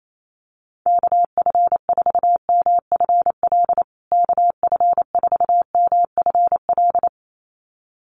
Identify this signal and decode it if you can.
Morse code, heard as a keyed tone, about 28 wpm, 700 Hz. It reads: KF4MFL KF4MFL
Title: USA Call Signs Repeated 2x 28wpm